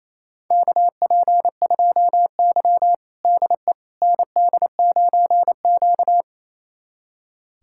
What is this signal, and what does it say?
Morse code, heard as a keyed tone, about 28 wpm, 700 Hz. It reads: KP2Y DE ND9Q
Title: USA Callsign DE Callsign - 4 Characters - 28wpm